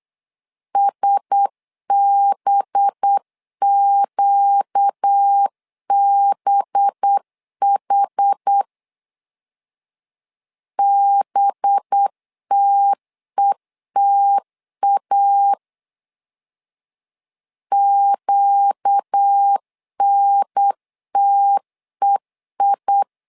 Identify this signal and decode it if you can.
Morse code, heard as a keyed tone, about 8 wpm, 800 Hz. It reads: SBQBH BTETA QNTEI
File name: LEZIONI / 5 LBVC / 5 LB 8wpm 2816